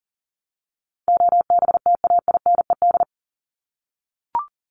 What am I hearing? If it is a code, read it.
Morse code, heard as a keyed tone, about 40 wpm, 700 Hz. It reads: OBTAINED
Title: ED Suffix - Top 100 Words with Base Words 40wpm